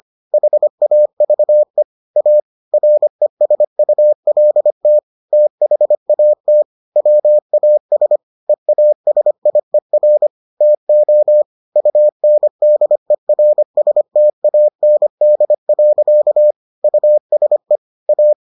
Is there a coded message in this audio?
Yes: Morse code from 0.3 to 18.3 s, heard as a keyed tone, about 25 wpm, 600 Hz. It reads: HAVE A RESULT THAT WAS EASIER TO UNDERSTAND. USE A